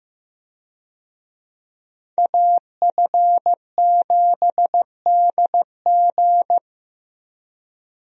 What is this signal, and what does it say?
Morse code, heard as a keyed tone, about 16 wpm, 700 Hz. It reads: AF7DG
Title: USA Call Signs - 5 Characters - Repeated 2x - 15wpm